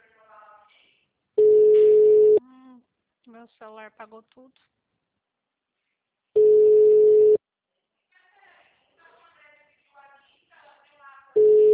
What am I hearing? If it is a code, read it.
Ringback tone (Europe, 425 Hz), with speech.